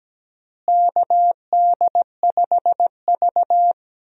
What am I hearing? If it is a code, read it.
Morse code, heard as a keyed tone, about 17 wpm, 700 Hz. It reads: KD5V
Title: USA Call Signs - 4 Characters - Repeated 3x - 17wpm